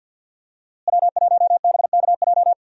Morse code, heard as a keyed tone, about 47 wpm, 700 Hz. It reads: W1BKJ
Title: USA Call Signs - 5 Characters - Repeated 2x - 50wpm